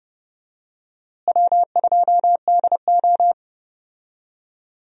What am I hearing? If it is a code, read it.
Morse code, heard as a keyed tone, about 30 wpm, 700 Hz. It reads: W2DO
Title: USA Call Signs - 4 Characters - Repeated 3x - 30wpm